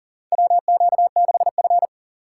Morse code, heard as a keyed tone, about 40 wpm, 700 Hz. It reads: WQ6F
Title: USA Call Signs - 2x Character Spacing - 40wpm